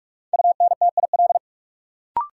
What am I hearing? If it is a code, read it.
Morse code, heard as a keyed tone, about 46 wpm, 700 Hz. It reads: UNTIL
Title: Sentences from Top 500 Words - New Word Drill - 45wpm